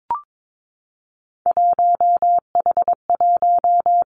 Morse code, heard as a keyed tone, about 22 wpm, 700 Hz. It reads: 1H1
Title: Three Challenging Letter-Number Combinations 22wpm